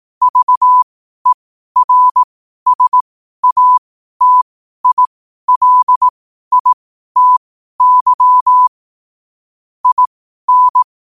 Morse code, heard as a keyed tone, about 18 wpm, spaced as for 12 wpm, 1000 Hz. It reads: VERSATILITY IN
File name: qotdfiles / 12wpm